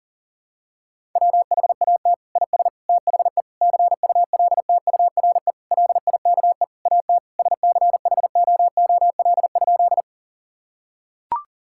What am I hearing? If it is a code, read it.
Morse code, heard as a keyed tone, about 40 wpm, 690 Hz. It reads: WHAT IS THE CULTURE LIKE AT SCHOOL?
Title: Sentences from Top 1000 Words - Review Part 6 - 40wpm